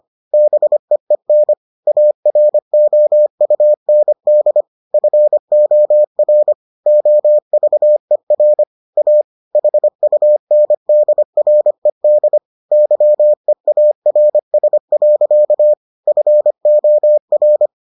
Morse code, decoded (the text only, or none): BEEN AROUND FOR OVER A HUNDRED YEARS. FOR